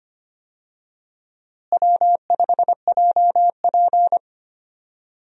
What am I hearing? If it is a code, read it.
Morse code, heard as a keyed tone, about 25 wpm, 700 Hz. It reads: W5JP
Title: USA Call Signs - 4 Characters - Repeated 3x - 25wpm